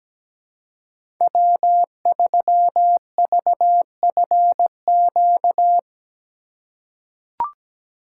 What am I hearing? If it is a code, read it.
Morse code, heard as a keyed tone, about 17 wpm, 700 Hz. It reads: W3VFQ
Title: USA Call Signs - 5 Characters - 3x Character Spacing - 17wpm